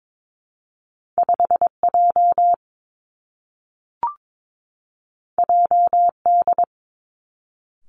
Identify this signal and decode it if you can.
Morse code, heard as a keyed tone, about 22 wpm, 700 Hz. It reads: 5 J J D
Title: Two Challenging Letter-Number Combinations 22wpm